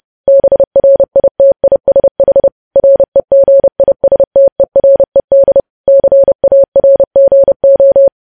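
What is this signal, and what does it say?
Morse code, heard as a keyed tone, about 30 wpm, 550 Hz. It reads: BRITISH REGISTERED CARGO
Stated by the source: Morse news headlines sent at 30.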